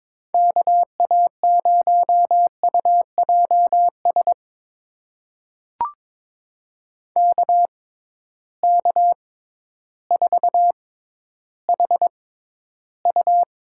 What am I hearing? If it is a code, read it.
Morse code, heard as a keyed tone, about 22 wpm, spaced as for 12 wpm, 700 Hz. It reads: KA0UJS KK4HU
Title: USA Call Signs - 6 Characters - 6x Character Spacing - 22wpm